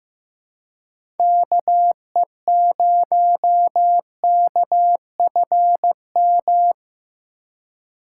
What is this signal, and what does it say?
Morse code, heard as a keyed tone, about 15 wpm, 700 Hz. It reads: KE0KFM
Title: USA Call Signs - 4x Character Spacing - 15wpm